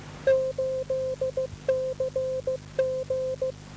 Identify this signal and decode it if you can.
Morse code, heard as a keyed tone, about 15 wpm, 530 Hz. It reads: 8CG
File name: DAN 12898.5 1139